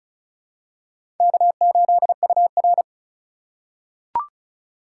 Morse code, heard as a keyed tone, about 36 wpm, 700 Hz. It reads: K8UR
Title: USA Call Signs - 4 Characters - 2x Character Spacing - 35wpm